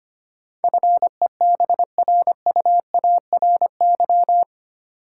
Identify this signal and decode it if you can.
Morse code, heard as a keyed tone, about 25 wpm, 700 Hz. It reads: FEBRUARY
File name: Eight Letter Words - 25wpm